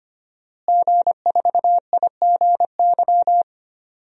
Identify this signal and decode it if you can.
Morse code, heard as a keyed tone, about 25 wpm, 700 Hz. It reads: G4IGY
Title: DXCC-100 Call Signs - 5 Characters - Repeated 2x - 25wpm